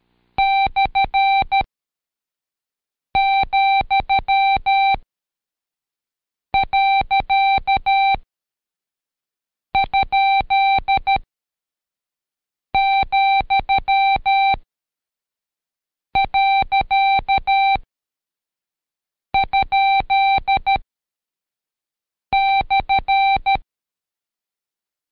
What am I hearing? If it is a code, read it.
Morse code, heard as a keyed tone, about 13 wpm, spaced as for 5 wpm, 780 Hz. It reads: /,.?,.?/
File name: Code Quick / AudioDownld2 / Seg101-Day12-Review Punc